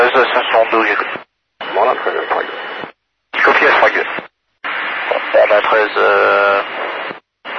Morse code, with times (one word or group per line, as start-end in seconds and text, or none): none